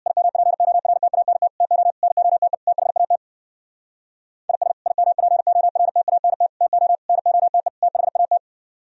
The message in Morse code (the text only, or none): IMPORTANT TO NOTE THAT IS IMPORTANT TO NOTE THAT